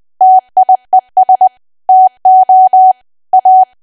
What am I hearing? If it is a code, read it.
Morse code, heard as a keyed tone, about 20 wpm, 750 Hz. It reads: TIES TO A
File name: cwa / W1AW-Level3 / 20WPM